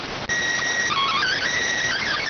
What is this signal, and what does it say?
SSTV (slow-scan television), PD 290.